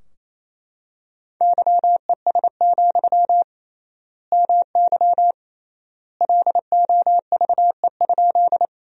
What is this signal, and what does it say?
Morse code, heard as a keyed tone, about 28 wpm, 700 Hz. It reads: YES, MY LOVE?
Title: 3x Word-Spacing - Sentences from Top 500 Words - 28wpm